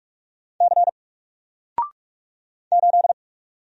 Morse code, heard as a keyed tone, about 45 wpm, 700 Hz. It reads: C 8
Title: Single Letter-Number Alt 45wpm